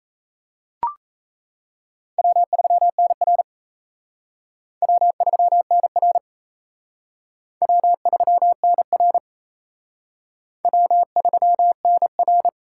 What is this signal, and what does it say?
Morse code, heard as a keyed tone, about 34 wpm, 700 Hz. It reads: W3NR W3NR W3NR W3NR
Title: Speed Racing - USA 1x2 Call Signs 28wpm